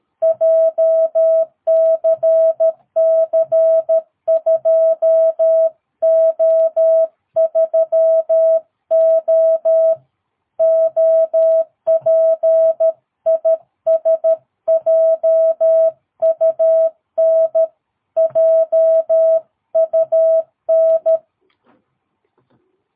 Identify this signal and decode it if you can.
Morse code, heard as a keyed tone, about 13 wpm, 650 Hz. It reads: JCC2O3O OP IS JUN JUN
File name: jh8cbh / 2022cw